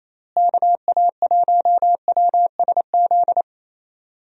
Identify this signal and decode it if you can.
Morse code, heard as a keyed tone, about 27 wpm, 700 Hz. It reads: KA1WSZ
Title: USA Call Signs - 6 Characters - Repeated 2x - 28wpm